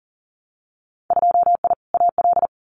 Morse code, heard as a keyed tone, about 41 wpm, 700 Hz. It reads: 2I AL
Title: Winter Field Day Exchanges 40wpm